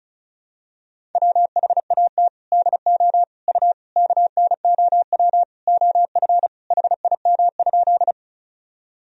Morse code, heard as a keyed tone, about 36 wpm, 700 Hz. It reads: WHAT DO U KNOW OF HIM?
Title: Sentences from Top 100 Words and Abbreviations 35wpm